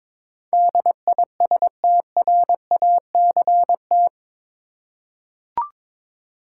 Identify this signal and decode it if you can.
Morse code, heard as a keyed tone, about 22 wpm, 700 Hz. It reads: DISTRACT
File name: practice / 22wpm / Eight Letter Words - 22wpm